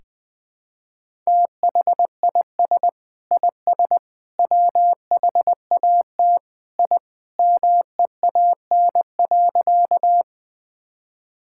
Morse code, heard as a keyed tone, about 20 wpm, 700 Hz. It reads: THIS IS WHAT I MEAN.
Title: Sentences with 3-Word Phrases - Spoken Beginning - 20wpm